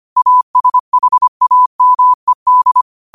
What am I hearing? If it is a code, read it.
Morse code, heard as a keyed tone, about 25 wpm, 1000 Hz. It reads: ASHAMED
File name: qotdfiles / 25wpm